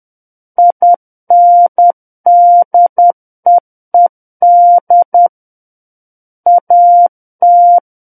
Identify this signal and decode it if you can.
Morse code, heard as a keyed tone, about 10 wpm, 700 Hz. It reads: INDEED AT
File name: Book One / 01-Chapter 1 - 10 WPM